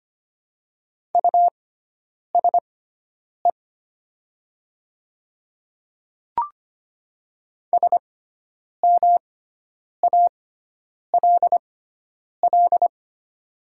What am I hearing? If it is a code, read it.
Morse code, heard as a keyed tone, about 25 wpm, spaced as for 9 wpm, 700 Hz. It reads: USE SMALL
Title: 6x Character Spacing - Top 300 Words - 25wpm